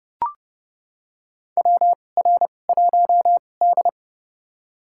Morse code, heard as a keyed tone, about 30 wpm, spaced as for 22 wpm, 700 Hz. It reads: WR1D
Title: USA Call Signs - 4 Characters - 2x Character Spacing - 30wpm